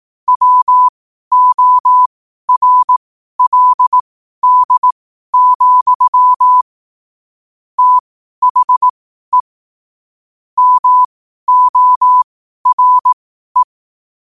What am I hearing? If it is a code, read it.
Morse code, heard as a keyed tone, about 18 wpm, spaced as for 12 wpm, 1000 Hz. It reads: WORLD, THE MORE
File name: qotdfiles / 12wpm